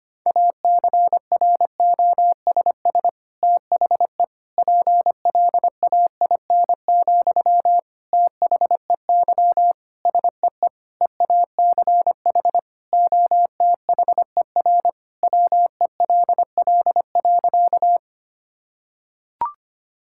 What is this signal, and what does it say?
Morse code, heard as a keyed tone, about 25 wpm, 700 Hz. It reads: ACROSS THE PLAIN, THEY SEE EACH OTHER WELL.
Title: Sentences from Top 1000 Words - Review Part 5 - 25wpm